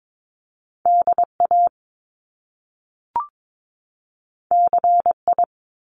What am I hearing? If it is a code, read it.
Morse code, heard as a keyed tone, about 22 wpm, 700 Hz. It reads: DA CI
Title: Top Two Letter Combinations in Words 22wpm